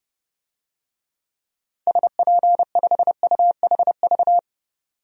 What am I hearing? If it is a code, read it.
Morse code, heard as a keyed tone, about 30 wpm, 700 Hz. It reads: SP5UHV
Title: DXCC-100 Call Signs - 6 Characters - 30wpm